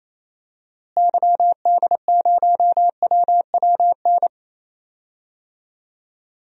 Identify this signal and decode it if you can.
Morse code, heard as a keyed tone, about 28 wpm, 700 Hz. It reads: YD0WWN
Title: DXCC-100 Call Signs - 6 Characters - Repeated 3x - 28wpm